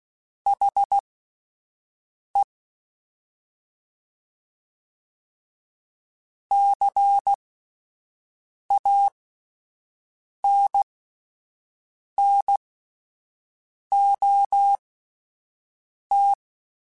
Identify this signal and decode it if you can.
Morse code, heard as a keyed tone, about 16 wpm, spaced as for 5 wpm, 800 Hz. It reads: HE CANNOT
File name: Spaced / Chuckles Spaced-mp3s / HalfFull sp @10